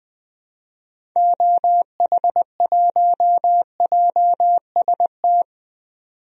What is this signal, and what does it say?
Morse code, heard as a keyed tone, about 20 wpm, 700 Hz. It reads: OH1JST
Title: DXCC-100 Call Signs - 6 Characters - 20wpm